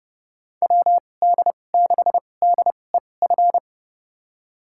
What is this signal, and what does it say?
Morse code, heard as a keyed tone, about 30 wpm, spaced as for 22 wpm, 700 Hz. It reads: WD6DEF
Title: USA Call Signs - 2x Character Spacing - 30wpm